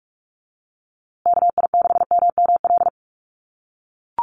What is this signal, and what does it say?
Morse code, heard as a keyed tone, about 44 wpm, 700 Hz. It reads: KI6MML
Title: USA Call Signs Repeated 3x 45wpm